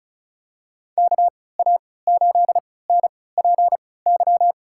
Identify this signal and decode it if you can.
Morse code, heard as a keyed tone, about 36 wpm, spaced as for 20 wpm, 700 Hz. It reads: KA8NPY
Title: USA Call Signs - 6 Characters - 3x Character Spacing - 35wpm